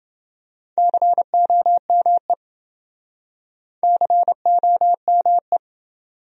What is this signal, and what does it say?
Morse code, heard as a keyed tone, about 30 wpm, 700 Hz. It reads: COME COME